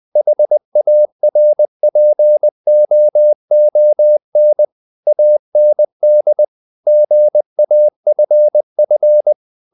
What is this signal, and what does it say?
Morse code, heard as a keyed tone, about 20 wpm, 600 Hz. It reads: HARPOON AND GAFF
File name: the old man and the sea / the old man and the sea part 1 20wpm